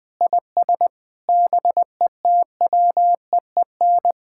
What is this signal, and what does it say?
Morse code, heard as a keyed tone, about 20 wpm, 700 Hz. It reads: IS BETWEEN
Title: Sentences from Top 1600 Words 20wpm